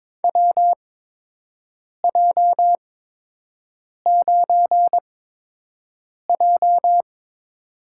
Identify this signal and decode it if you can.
Morse code, heard as a keyed tone, about 22 wpm, spaced as for 6 wpm, 700 Hz. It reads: WJ9J
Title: USA Call Signs - 4 Characters - 8x Character Spacing - 22wpm